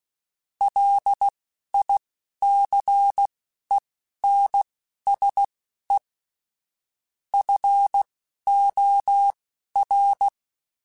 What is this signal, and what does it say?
Morse code, heard as a keyed tone, about 16 wpm, spaced as for 11 wpm, 800 Hz. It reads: LICENSE FOR
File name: QSOs / QSOs-mp3s / kc7haa@10